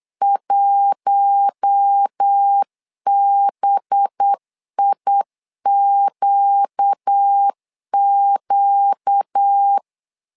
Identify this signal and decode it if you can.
Morse code, heard as a keyed tone, about 8 wpm, 800 Hz. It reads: 1BIQQ
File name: LEZIONI / 5 LBVC / 5 LB 8wpm 2816